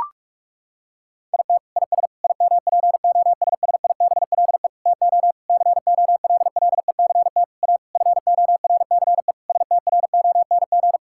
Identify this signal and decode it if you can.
Morse code, heard as a keyed tone, about 42 wpm, 700 Hz. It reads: IT IS IMPOSSIBLE TO COLLECT A FORCE STRONG
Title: Sentences from Top 1600 Words 45wpm